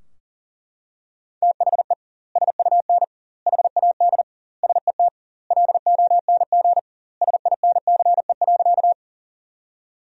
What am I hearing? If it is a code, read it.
Morse code, heard as a keyed tone, about 40 wpm, 700 Hz. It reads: THE SUN HAD SET LONG SINCE.
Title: 2x Word-Spacing - Sentences from Top 300 Words - 40wpm